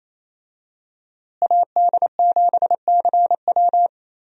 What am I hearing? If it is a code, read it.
Morse code, heard as a keyed tone, about 28 wpm, 700 Hz. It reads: AD7CW
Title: USA Call Signs - 5 Characters - Repeated 3x - 28wpm